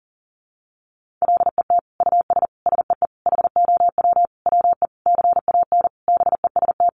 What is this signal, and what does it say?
Morse code, heard as a keyed tone, about 40 wpm, 700 Hz. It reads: LET US SEE HOW WE CAN BEST
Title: Sentences from Top 1400 Words 40wpm